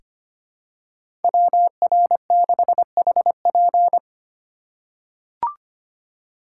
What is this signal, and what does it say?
Morse code, heard as a keyed tone, about 25 wpm, 700 Hz. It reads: WR6HP
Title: USA Call Signs - 5 Characters - 2x Character Spacing - 25wpm